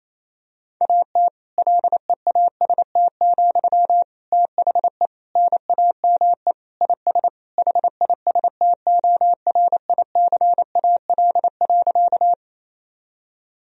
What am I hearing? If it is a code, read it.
Morse code, heard as a keyed tone, about 28 wpm, 700 Hz. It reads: AT LEAST, THE NAME IS HISTORICAL.
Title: Sentences from Top 1000 Words - Review Part 4 - 28wpm